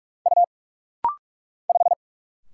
Morse code, heard as a keyed tone, about 46 wpm, 700 Hz. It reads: U 5
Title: Single Letter-Number Alt - Rapid-Fire 45wpm